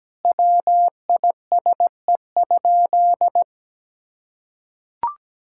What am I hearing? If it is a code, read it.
Morse code, heard as a keyed tone, about 17 wpm, 700 Hz. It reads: WISE?
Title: Sentences from Top 1600 Words 17wpm